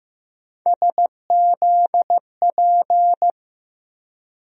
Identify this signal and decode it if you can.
Morse code, heard as a keyed tone, about 15 wpm, 700 Hz. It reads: SZP
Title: USA Call Sign Postfixes - 3 Letters 15wpm